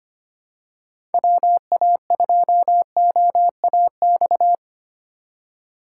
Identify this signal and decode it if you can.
Morse code, heard as a keyed tone, about 25 wpm, 700 Hz. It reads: WA2OAX
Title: USA Call Signs - 6 Characters - 4x Character Spacing - 25wpm